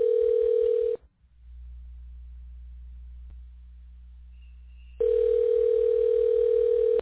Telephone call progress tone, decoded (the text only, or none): ringback tone